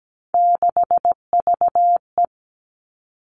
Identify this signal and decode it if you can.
Morse code, heard as a keyed tone, about 17 wpm, 700 Hz. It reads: 6VE
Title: Three Challenging Letter-Number Combinations 17wpm